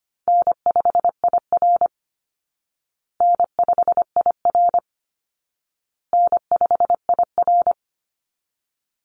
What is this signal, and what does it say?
Morse code, heard as a keyed tone, about 25 wpm, 700 Hz. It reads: N5IR N5IR N5IR
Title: Call Signs Repeated 2x - 25wpm